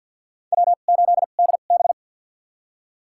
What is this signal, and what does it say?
Morse code, heard as a keyed tone, about 53 wpm, spaced as for 37 wpm, 700 Hz. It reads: W8DB
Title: USA Call Signs - 4 Characters - 2x Character Spacing - 50wpm